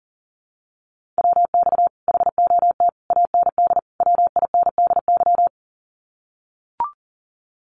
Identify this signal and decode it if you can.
Morse code, heard as a keyed tone, about 40 wpm, 700 Hz. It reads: WX HOT AND WINDY
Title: Common QSO Phrases Encore 40wpm